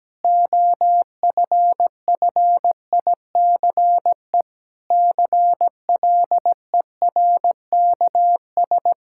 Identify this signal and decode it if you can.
Morse code, heard as a keyed tone, about 17 wpm, 700 Hz. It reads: OFFICE CLERKS
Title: Pangram Challenge v2 - 17wpm